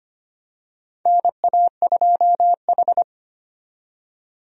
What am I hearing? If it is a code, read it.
Morse code, heard as a keyed tone, about 25 wpm, 700 Hz. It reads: NA2H
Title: USA Call Signs - 4 Characters - 25wpm